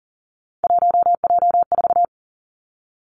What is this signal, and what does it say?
Morse code, heard as a keyed tone, about 40 wpm, 700 Hz. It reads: 1J4
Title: Three Challenging Letter-Number Combinations 40wpm